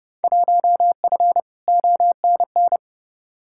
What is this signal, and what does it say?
Morse code, heard as a keyed tone, about 31 wpm, 700 Hz. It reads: 1F ONN
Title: Field Day Exchanges Encore - 30wpm